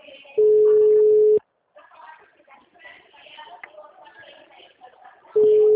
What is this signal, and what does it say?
Ringback tone (Europe, 425 Hz), with speech.
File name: out- -2017- - - .56960